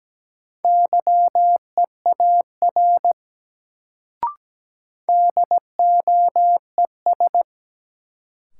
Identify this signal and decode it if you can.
Morse code, heard as a keyed tone, about 17 wpm, 700 Hz. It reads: YEAR DOES
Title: Sentences from Top 200 Words - New Word Drill - 17wpm